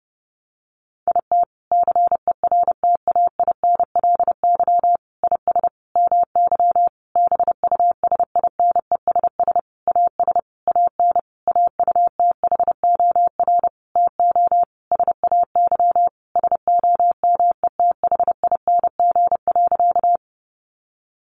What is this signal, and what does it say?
Morse code, heard as a keyed tone, about 30 wpm, 700 Hz. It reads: IT CERTAINLY IS MY BUSINESS AS AN AUTHOR TO SAY SOMETHING.
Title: Sentences from Top 1200 Words 30wpm